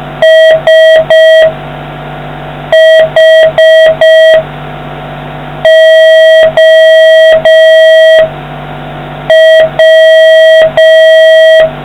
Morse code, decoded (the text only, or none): SHOW